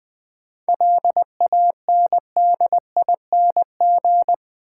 Morse code, heard as a keyed tone, about 20 wpm, 700 Hz. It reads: LANDING
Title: 7 Letter Words - First 3 Letters Given - 20wpm